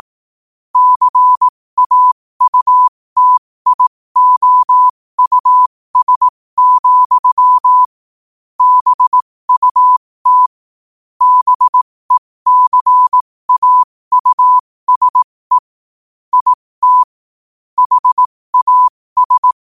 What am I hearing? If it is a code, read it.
Morse code, heard as a keyed tone, about 18 wpm, spaced as for 15 wpm, 1000 Hz. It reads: CAUTIOUS, BUT BECAUSE IT HAS
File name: qotdfiles / 15wpm